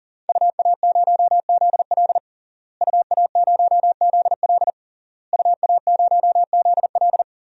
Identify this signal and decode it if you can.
Morse code, heard as a keyed tone, about 40 wpm, 700 Hz. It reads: UA0ZL UA0ZL UA0ZL
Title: DXCC-100 Call Signs - 5 Characters - Repeated 3x - 40wpm